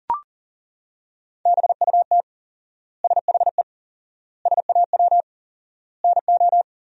Morse code, heard as a keyed tone, about 40 wpm, 700 Hz. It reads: BUT SHE SAW NO
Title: 4x Word-Spacing - Sentences from Top 300 Words - 40wpm